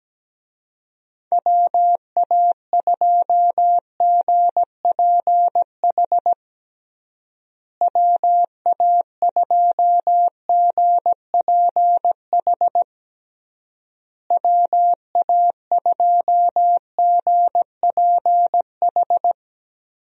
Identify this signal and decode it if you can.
Morse code, heard as a keyed tone, about 17 wpm, 700 Hz. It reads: WA2GPH WA2GPH WA2GPH
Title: USA Call Signs - 6 Characters - Repeated 3x - 17wpm